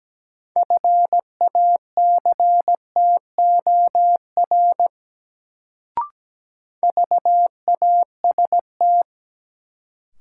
Morse code, heard as a keyed tone, about 17 wpm, 700 Hz. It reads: FACTOR VAST
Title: Sentences from Top 1300 Words - New Word Drill - 17wpm